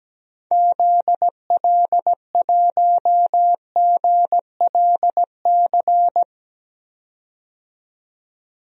Morse code, heard as a keyed tone, about 17 wpm, 700 Hz. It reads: ZL1GLC
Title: DXCC-100 Call Signs - 6 Characters - 17wpm